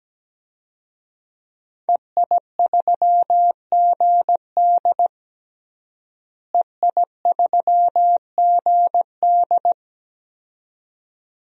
Morse code, heard as a keyed tone, about 17 wpm, 700 Hz. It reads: EI3GD EI3GD
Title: DXCC-100 Call Signs - 5 Characters - Repeated 2x - 17wpm